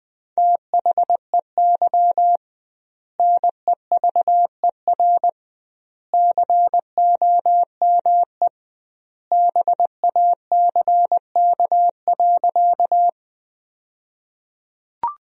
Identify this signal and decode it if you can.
Morse code, heard as a keyed tone, about 21 wpm, 700 Hz. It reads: THEY NEVER COME BACK.
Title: 2x Word-Spacing - Sentences from Top 300 Words - 20wpm